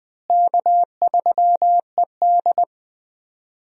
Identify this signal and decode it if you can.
Morse code, heard as a keyed tone, about 20 wpm, 700 Hz. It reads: K3ED
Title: USA Call Signs - 4 Characters - Repeated 3x - 20wpm